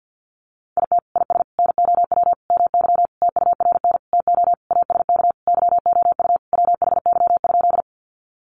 Morse code, heard as a keyed tone, about 50 wpm, 700 Hz. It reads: IT IS NOW MY TURN TO ASK YOU WHY?
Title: Sentences from Top 200 Words 50wpm